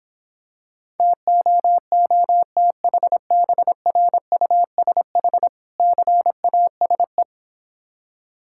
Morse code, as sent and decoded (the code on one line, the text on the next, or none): - --- --- - .... -... .-. ..- ... .... / -.-. .- ... .
TOOTHBRUSH CASE